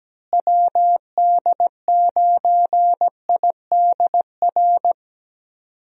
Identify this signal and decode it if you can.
Morse code, heard as a keyed tone, about 17 wpm, 700 Hz. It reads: WD9IDR
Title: USA Call Signs - 6 Characters - Repeated 2x - 17wpm